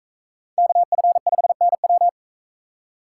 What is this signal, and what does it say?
Morse code, heard as a keyed tone, about 38 wpm, 700 Hz. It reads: KF5NW